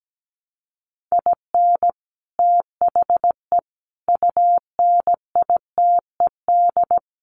Morse code, heard as a keyed tone, about 17 wpm, 700 Hz. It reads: IN THE UNITED
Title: Sets of 5 Words 17wpm